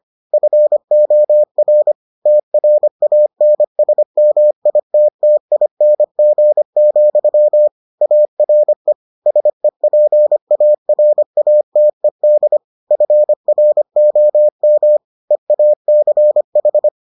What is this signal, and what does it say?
Morse code, heard as a keyed tone, about 25 wpm, 600 Hz. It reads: FOR TRANSMITTING, ARE SEPARATED FROM EACH